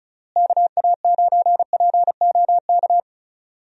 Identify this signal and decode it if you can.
Morse code, heard as a keyed tone, about 34 wpm, 700 Hz. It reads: KA9POK
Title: USA Call Signs - 6 Characters - Repeated 3x - 35wpm